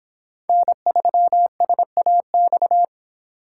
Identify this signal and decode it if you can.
Morse code, heard as a keyed tone, about 23 wpm, 700 Hz. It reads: N3SAX